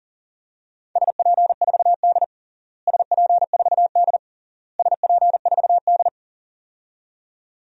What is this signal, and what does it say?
Morse code, heard as a keyed tone, about 37 wpm, 700 Hz. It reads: SP4D SP4D SP4D
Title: DXCC-100 Call Signs - 4 Characters - Repeated 3x - 40wpm